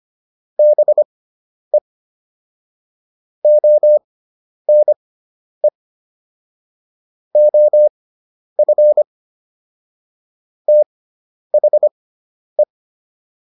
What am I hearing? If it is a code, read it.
Morse code, heard as a keyed tone, about 25 wpm, spaced as for 10 wpm, 600 Hz. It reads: BE ONE OF THE
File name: WD304 10